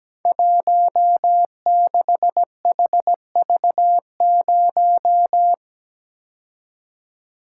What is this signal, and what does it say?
Morse code, heard as a keyed tone, about 17 wpm, 700 Hz. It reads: 16HV0
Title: Code Groups - 5 Letter:Numbers - 17wpm